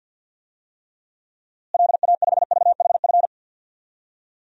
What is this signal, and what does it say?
Morse code, heard as a keyed tone, about 52 wpm, 700 Hz. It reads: LA5VHF
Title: DXCC-100 Call Signs - 6 Characters - Repeated 3x - 50wpm